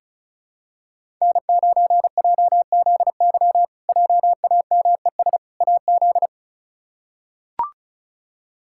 Morse code, heard as a keyed tone, about 34 wpm, 700 Hz. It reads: N9JZY JAMES AZ
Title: K1USN SST QSO Practice - 35wpm